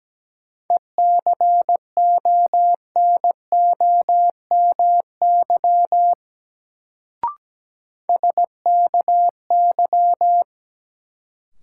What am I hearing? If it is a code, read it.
Morse code, heard as a keyed tone, about 17 wpm, 700 Hz. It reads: ECONOMY SKY
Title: Sentences from Top 1200 Words - New Word Drill Alt - 17wpm